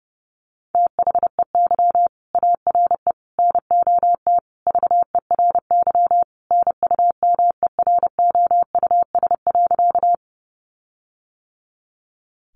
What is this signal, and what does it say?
Morse code, heard as a keyed tone, about 30 wpm, 700 Hz. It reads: THEY ARE NOT VERY NUMEROUS.
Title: Sentences from Top 1400 Words 30wpm